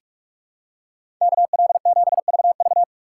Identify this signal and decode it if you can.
Morse code, heard as a keyed tone, about 44 wpm, 700 Hz. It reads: KL7VV
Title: USA Call Signs - 5 Characters - 6x Character Spacing - 45wpm